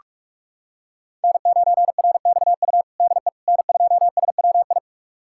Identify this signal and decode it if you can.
Morse code, heard as a keyed tone, about 46 wpm, 700 Hz. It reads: N9RXU DE N2SWI
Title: USA Callsign DE Callsign - 5 Characters - 45wpm